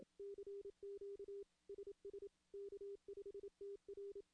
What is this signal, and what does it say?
Morse code, heard as a keyed tone, about 27 wpm, 400 Hz. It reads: CQ SS K5TR
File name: contests / CW-SS-k5tr- -1600